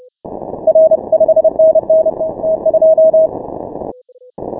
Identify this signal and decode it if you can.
Morse code, heard as a keyed tone, about 30 wpm, 640 Hz. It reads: R 5NN T2
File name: final / W1UJ-168-140-MorseRunner